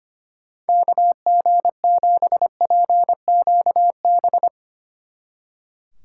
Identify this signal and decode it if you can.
Morse code, heard as a keyed tone, about 33 wpm, 700 Hz. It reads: KG7PQB